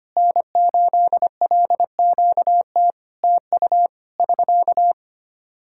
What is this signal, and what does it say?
Morse code, heard as a keyed tone, about 25 wpm, 700 Hz. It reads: N8LQT TU <SK>
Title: Practice QSO - Exchange 4 - Ending - 25wpm